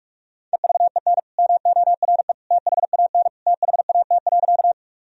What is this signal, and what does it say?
Morse code, heard as a keyed tone, about 41 wpm, 700 Hz. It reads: EVEN MORE THAN THAT.
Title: Sentences from Top 1000 Words - Review Part 1 - 45wpm